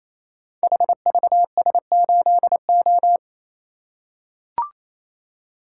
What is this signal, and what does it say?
Morse code, heard as a keyed tone, about 27 wpm, 700 Hz. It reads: HVS8O
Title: Code Groups - 5 Letter:Numbers - 28wpm